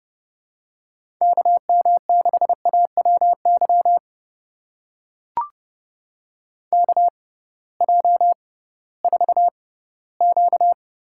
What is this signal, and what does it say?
Morse code, heard as a keyed tone, about 30 wpm, spaced as for 17 wpm, 700 Hz. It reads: KM6AWY KJ4Q
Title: USA Call Signs - 6 Characters - 6x Character Spacing - 30wpm